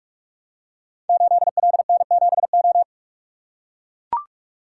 Morse code, heard as a keyed tone, about 44 wpm, 700 Hz. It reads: 8LN7O
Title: Code Groups - 5 Letter:Numbers - 45wpm